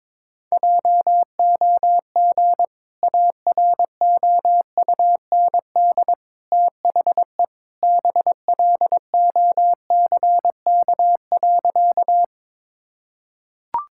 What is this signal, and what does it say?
Morse code, heard as a keyed tone, about 22 wpm, 700 Hz. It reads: JOG AROUND THE BLOCK.
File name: practice / 22wpm / Easy Sentences with Hints - 22wpm